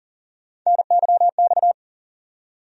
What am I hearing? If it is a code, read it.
Morse code, heard as a keyed tone, about 40 wpm, 700 Hz. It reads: NYX
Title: USA Call Sign Postfixes - 3 Letters 40wpm